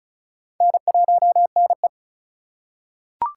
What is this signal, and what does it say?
Morse code, heard as a keyed tone, about 36 wpm, 700 Hz. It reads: N1NE
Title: USA Call Signs - 2x Character Spacing - 35wpm